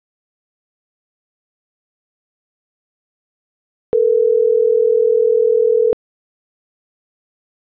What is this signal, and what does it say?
Ringback tone (North America), 440 + 480 Hz.